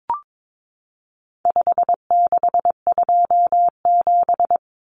Morse code, heard as a keyed tone, about 22 wpm, 700 Hz. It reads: 5627
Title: Common Cut Numbers - 4-Digit 22wpm